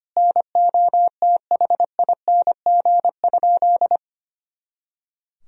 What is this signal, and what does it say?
Morse code, heard as a keyed tone, about 25 wpm, 700 Hz. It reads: NOTHING?
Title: Sentences from Top 500 Words and Abbreviations 25wpm